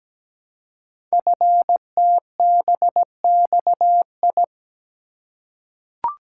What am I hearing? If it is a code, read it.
Morse code, heard as a keyed tone, about 17 wpm, 700 Hz. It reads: FTBXI
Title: Code Groups - 5 Letters - 17wpm